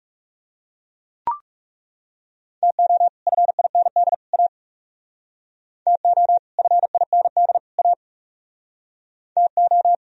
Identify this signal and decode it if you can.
Morse code, heard as a keyed tone, about 37 wpm, 700 Hz. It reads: TO FIND A TO FIND A TO
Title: Speed-Racing - Sets of 3 Words 30wpm